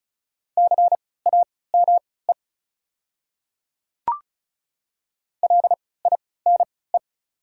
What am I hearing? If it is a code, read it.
Morse code, heard as a keyed tone, about 36 wpm, spaced as for 20 wpm, 700 Hz. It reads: CAME LINE
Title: 3x Character Spacing - Top 300 Words - 35wpm